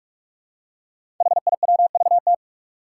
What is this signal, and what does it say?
Morse code, heard as a keyed tone, about 44 wpm, 700 Hz. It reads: HIWVT
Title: Five Challenging Letter-Number Combinations 45wpm